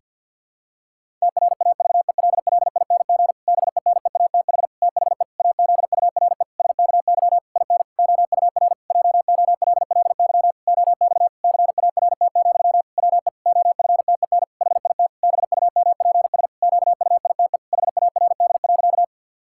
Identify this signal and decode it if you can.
Morse code, heard as a keyed tone, about 47 wpm, 700 Hz. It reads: TRAVELLING BENEATH THE AZURE SKY IN OUR JOLLY OX CART, WE OFTEN HIT BUMPS QUITE HARD.
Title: Pangram Challenge v2 - 50wpm